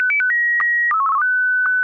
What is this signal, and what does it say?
SSTV (slow-scan television), Martin 1.
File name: martin1